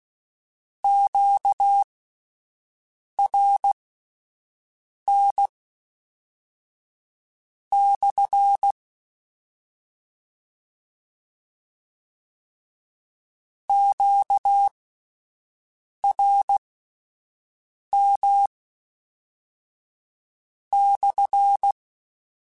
Morse code, heard as a keyed tone, about 16 wpm, spaced as for 5 wpm, 800 Hz. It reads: QRN/ QRM/
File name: Q Calls / Q Calls-mp3s / Q Calls SPSL1 @10